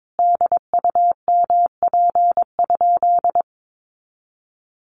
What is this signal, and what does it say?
Morse code, heard as a keyed tone, about 22 wpm, 700 Hz. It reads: DUMP?
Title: Sentences from Top 1300 Words 22wpm